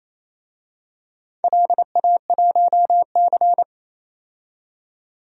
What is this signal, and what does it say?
Morse code, heard as a keyed tone, about 28 wpm, 700 Hz. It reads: LA1C
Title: DXCC-100 Call Signs - 4 Characters - 28wpm